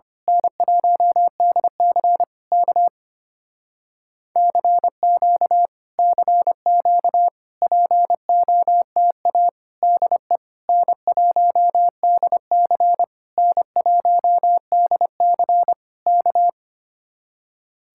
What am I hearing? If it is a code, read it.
Morse code, heard as a keyed tone, about 24 wpm, 700 Hz. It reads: N1DC K CQ CQ POTA DE N1DC N1DC K
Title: Speed-Racing Common QSO Phrases - 25wpm